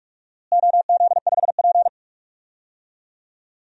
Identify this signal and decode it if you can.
Morse code, heard as a keyed tone, about 47 wpm, 700 Hz. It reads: OZ5P
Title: DXCC-100 Call Signs - 4 Characters - Repeated 2x - 45wpm